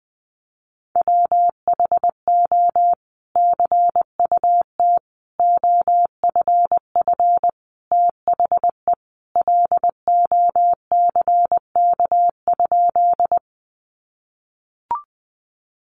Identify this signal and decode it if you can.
Morse code, heard as a keyed tone, about 20 wpm, 700 Hz. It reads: WHO CUT OFF THE LOCK?
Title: Sentences from Top 1300 Words 20wpm